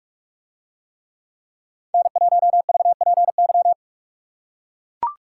Morse code, heard as a keyed tone, about 44 wpm, 700 Hz. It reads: N1VPY